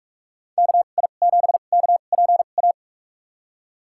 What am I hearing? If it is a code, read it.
Morse code, heard as a keyed tone, about 46 wpm, spaced as for 32 wpm, 700 Hz. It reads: KI7KPA
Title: USA Call Signs - 6 Characters - 2x Character Spacing - 45wpm